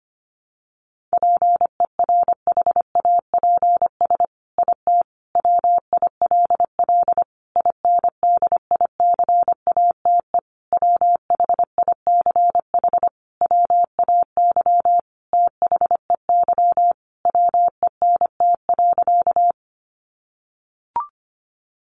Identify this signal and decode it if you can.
Morse code, heard as a keyed tone, about 25 wpm, 700 Hz. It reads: PERHAPS IT WILL INDICATE WHICH WAY THEY WENT.
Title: Sentences from Top 1300 Words 25wpm